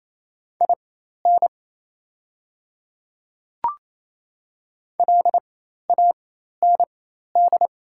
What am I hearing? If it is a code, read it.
Morse code, heard as a keyed tone, about 28 wpm, spaced as for 13 wpm, 700 Hz. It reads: IN LAND
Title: 4x Character Spacing - Top 200 Words - 28wpm